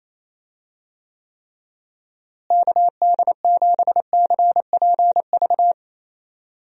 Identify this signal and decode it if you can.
Morse code, heard as a keyed tone, about 28 wpm, 700 Hz. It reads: KD7CPV
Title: USA Call Signs - 6 Characters - Repeated 2x - 28wpm